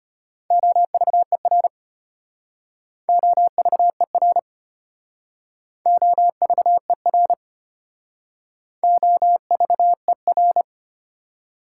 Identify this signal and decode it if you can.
Morse code, heard as a keyed tone, about 31 wpm, 700 Hz. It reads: OVER OVER OVER OVER
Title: Speed-Racing Prefixes and Suffixes 25wpm